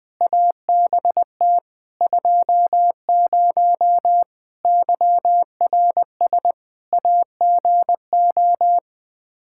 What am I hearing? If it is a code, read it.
Morse code, heard as a keyed tone, about 21 wpm, 700 Hz. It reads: ABT 20 YRS AGO
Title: Practice QSO - Exchange 3 - More Info - 20wpm